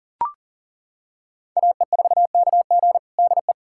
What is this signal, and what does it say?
Morse code, heard as a keyed tone, about 40 wpm, 700 Hz. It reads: AE4KG DE
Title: USA Callsign DE Callsign - 5 Characters - 40wpm